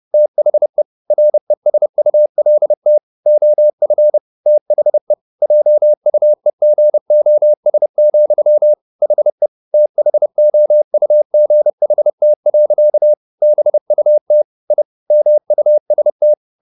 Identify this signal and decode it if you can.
Morse code, heard as a keyed tone, about 30 wpm, 600 Hz. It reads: THE RESULT OF THE JUEGOS, HE THOUGHT. BUT I MUST
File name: the old man and the sea / the old man and the sea 1 30wpm